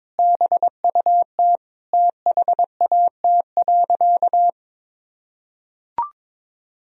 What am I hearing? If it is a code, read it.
Morse code, heard as a keyed tone, about 22 wpm, 700 Hz. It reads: BUT THAT.
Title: Sentences from Top 1000 Words - Review Part 3 - 22wpm